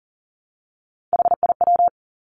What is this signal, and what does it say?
Morse code, heard as a keyed tone, about 38 wpm, 700 Hz. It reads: HIW
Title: Three Challenging Letter-Number Combinations 40wpm